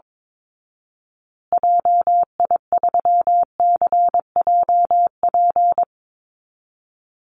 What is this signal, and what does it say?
Morse code, heard as a keyed tone, about 22 wpm, 700 Hz. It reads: JI3CJP
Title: Call Signs Repeated 3x - 22wpm